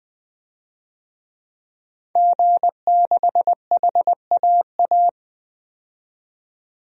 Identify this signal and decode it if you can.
Morse code, heard as a keyed tone, about 20 wpm, 700 Hz. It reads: G6HAA
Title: DXCC-100 Call Signs - 5 Characters - 20wpm